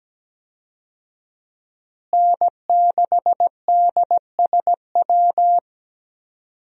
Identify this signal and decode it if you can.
Morse code, heard as a keyed tone, about 17 wpm, 700 Hz. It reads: N6DSW
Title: USA Call Signs - 5 Characters - Repeated 3x - 17wpm